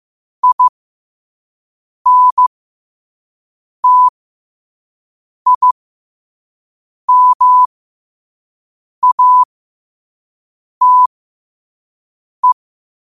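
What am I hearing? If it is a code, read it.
Morse code, heard as a keyed tone, about 15 wpm, spaced as for 5 wpm, 1000 Hz. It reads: INTIMATE
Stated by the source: Morse quote-of-the-day practice at 5 wpm.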